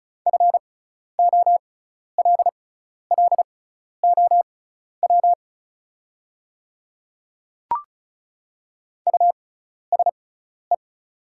Morse code, heard as a keyed tone, about 34 wpm, spaced as for 12 wpm, 700 Hz. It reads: FOLLOW USE
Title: 6x Character Spacing - Top 200 Words - 35wpm